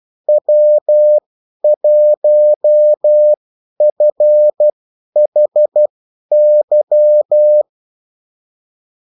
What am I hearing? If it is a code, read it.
Morse code, heard as a keyed tone, about 12 wpm, spaced as for 10 wpm, 600 Hz. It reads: W1FHY
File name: morse mp3 / revision9-vitesse 10 mots